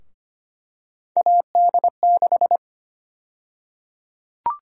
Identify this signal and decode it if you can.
Morse code, heard as a keyed tone, about 25 wpm, 700 Hz. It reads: AD6
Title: USA Call Sign Prefixes - 25wpm